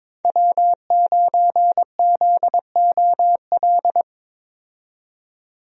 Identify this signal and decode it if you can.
Morse code, heard as a keyed tone, about 22 wpm, 700 Hz. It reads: W9ZOL
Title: USA Call Signs - 5 Characters - 22wpm